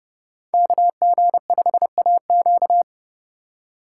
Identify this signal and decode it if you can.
Morse code, heard as a keyed tone, about 38 wpm, 700 Hz. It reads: KG5AQ